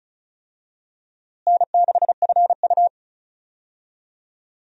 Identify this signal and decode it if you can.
Morse code, heard as a keyed tone, about 36 wpm, 700 Hz. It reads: N6FU
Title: USA Call Signs - 4 Characters - Repeated 3x - 35wpm